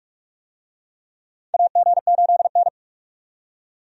Morse code, heard as a keyed tone, about 44 wpm, 700 Hz. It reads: AG8N
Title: USA Call Signs - 4 Characters - 45wpm